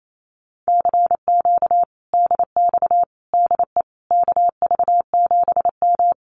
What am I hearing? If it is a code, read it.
Morse code, heard as a keyed tone, about 28 wpm, 700 Hz. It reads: CQ DX DE KV7M
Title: Common QSO Phrases Encore 28wpm